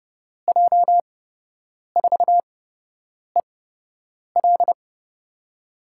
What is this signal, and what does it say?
Morse code, heard as a keyed tone, about 30 wpm, spaced as for 8 wpm, 700 Hz. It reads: J4EL
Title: USA Call Signs - 5 Characters - 8x Character Spacing - 30wpm